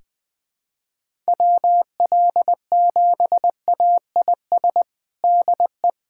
Morse code, heard as a keyed tone, about 20 wpm, 700 Hz. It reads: WL7AIS DE
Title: USA Callsign DE Callsign 20wpm